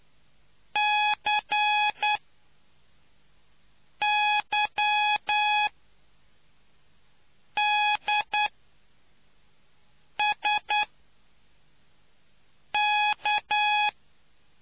Morse code, heard as a keyed tone, about 10 wpm, spaced as for 4 wpm, 840 Hz. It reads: CYDSK